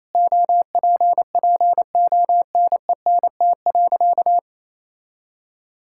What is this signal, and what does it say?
Morse code, heard as a keyed tone, about 28 wpm, 700 Hz. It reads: OPPONENT.
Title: Pangram Challenge v2 - 28wpm